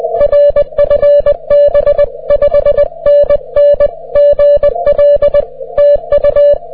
Morse code, heard as a keyed tone, about 20 wpm, 590 Hz. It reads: R FB 5NN GL TU
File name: wp / fk8hc 30m